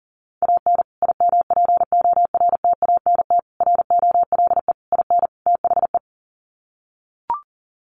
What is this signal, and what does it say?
Morse code, heard as a keyed tone, about 39 wpm, 700 Hz. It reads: AN IMPORTANT ROLE IN THE
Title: Sets of 5 Words Encore2 40wpm